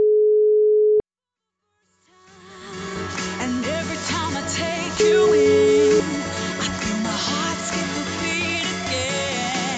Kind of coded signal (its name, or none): telephone call progress tone